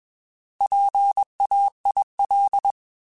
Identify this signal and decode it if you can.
Morse code, heard as a keyed tone, about 21 wpm, 800 Hz. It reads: PAIL
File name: Alphabetical / Alphabetical mp3s / Ps @20